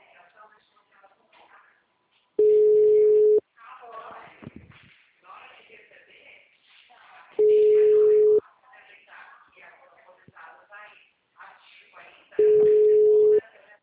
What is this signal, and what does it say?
Ringback tone (Europe, 425 Hz), with speech.